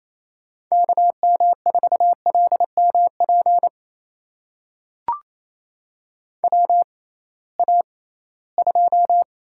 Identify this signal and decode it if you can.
Morse code, heard as a keyed tone, about 28 wpm, spaced as for 18 wpm, 700 Hz. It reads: KM4LMP WA2
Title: USA Call Signs - 6 Characters - 6x Character Spacing - 28wpm